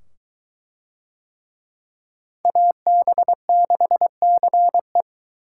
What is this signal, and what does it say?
Morse code, heard as a keyed tone, about 21 wpm, 700 Hz. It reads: AB6CE